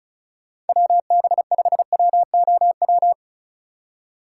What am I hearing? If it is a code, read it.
Morse code, heard as a keyed tone, about 44 wpm, 700 Hz. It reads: WB5WOW
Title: Speed Racing - USA 2x3 Call Signs 35wpm